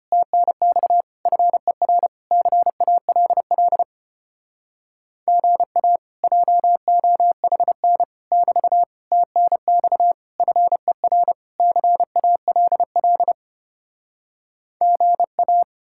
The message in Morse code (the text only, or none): TNX FER CALL GA JOHN <BT> TNX FER CALL GA